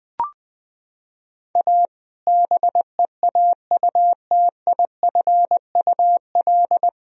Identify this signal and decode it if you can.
Morse code, heard as a keyed tone, about 20 wpm, 700 Hz. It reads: A BEAUTIFUL
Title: Sentences from Top 1000 Words - Review Part 1 - 20wpm